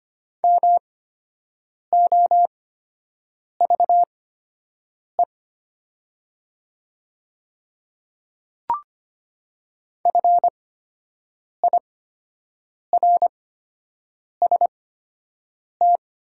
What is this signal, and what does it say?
Morse code, heard as a keyed tone, about 25 wpm, spaced as for 7 wpm, 700 Hz. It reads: MOVE FIRST
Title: 8x Character Spacing - Top 200 Words - 25wpm